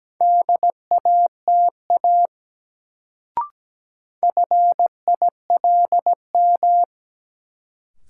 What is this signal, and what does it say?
Morse code, heard as a keyed tone, about 17 wpm, 700 Hz. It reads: DATA FILM
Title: Sentences from Top 1000 Words - Review Part 3 - New Word Drill - 17wpm